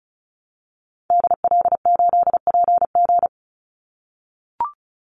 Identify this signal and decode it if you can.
Morse code, heard as a keyed tone, about 34 wpm, 700 Hz. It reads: DL8PG
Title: Call Signs - 35wpm